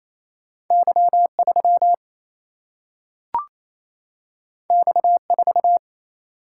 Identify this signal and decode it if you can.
Morse code, heard as a keyed tone, about 28 wpm, 700 Hz. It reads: Y 3 X 4
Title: Single-Letter Single-Number 28wpm